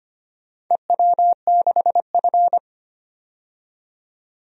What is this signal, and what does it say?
Morse code, heard as a keyed tone, about 25 wpm, 700 Hz. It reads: EW6F
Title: DXCC-100 Call Signs - 4 Characters - 25wpm